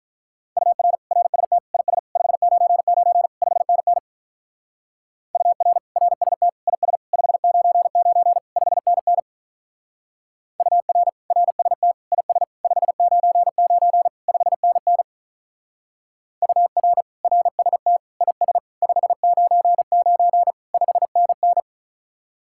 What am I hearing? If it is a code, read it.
Morse code, heard as a keyed tone, about 42 wpm, 700 Hz. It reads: UR RST IS 599 5NN UR RST IS 599 5NN UR RST IS 599 5NN UR RST IS 599 5NN
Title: Speed-Racing Common QSO Phrases - 35wpm